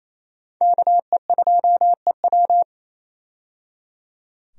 Morse code, heard as a keyed tone, about 28 wpm, 700 Hz. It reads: KE2EW
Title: USA Call Signs - 5 Characters - 28wpm